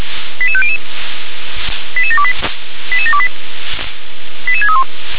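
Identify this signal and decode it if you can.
Five-tone selective call (ZVEI), heard as quick runs of five tones, tones of about 70 ms. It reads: 88500 88728 88728 88631
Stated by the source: ZVEI five-tone selective call received on SDR.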